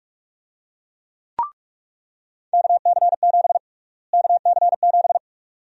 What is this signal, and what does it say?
Morse code, heard as a keyed tone, about 46 wpm, 700 Hz. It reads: KC7 KC7
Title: USA Call Sign Prefixes - Repeated 2x - 45wpm